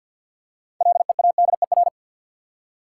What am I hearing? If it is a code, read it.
Morse code, heard as a keyed tone, about 48 wpm, 700 Hz. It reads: LEADER
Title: Six Letter Words - 50wpm